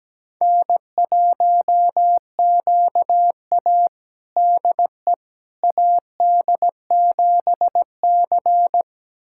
Morse code, heard as a keyed tone, about 17 wpm, 700 Hz. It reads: N1QA DE AD7C
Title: USA Callsign DE Callsign - 4 Characters - 17wpm